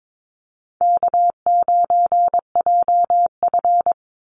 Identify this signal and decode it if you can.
Morse code, heard as a keyed tone, about 22 wpm, 700 Hz. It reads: K9JF
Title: Easy Call Signs Repeated 2x - 22wpm